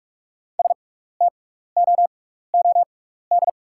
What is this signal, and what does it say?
Morse code, heard as a keyed tone, about 48 wpm, spaced as for 16 wpm, 690 Hz. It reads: STOOD
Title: 6x Character Spacing - Top 500 Words - 45wpm